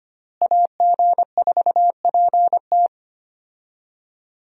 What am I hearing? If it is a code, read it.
Morse code, heard as a keyed tone, about 25 wpm, 700 Hz. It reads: AG4PT
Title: USA Call Signs - 5 Characters - Repeated 2x - 25wpm